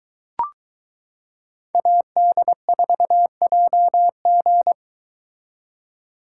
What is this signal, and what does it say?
Morse code, heard as a keyed tone, about 23 wpm, 700 Hz. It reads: AD4JG